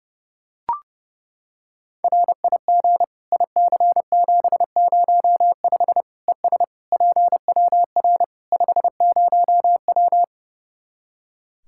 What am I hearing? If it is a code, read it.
Morse code, heard as a keyed tone, about 32 wpm, 700 Hz. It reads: RIG IC705 ES PWR 50W
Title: Practice QSO - Exchange 2 - Info - 30wpm